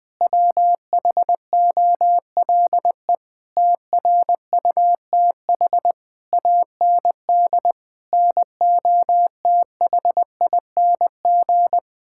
Morse code, heard as a keyed tone, about 21 wpm, 700 Hz. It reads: WHOLE TRUTH AND NOTHING
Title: Sentences with 9-Word Phrases - Spoken Beginning - 20wpm